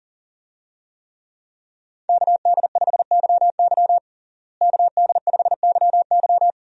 Morse code, heard as a keyed tone, about 40 wpm, 700 Hz. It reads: KD5YY KD5YY
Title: USA Call Signs - 5 Characters - Repeated 2x - 40wpm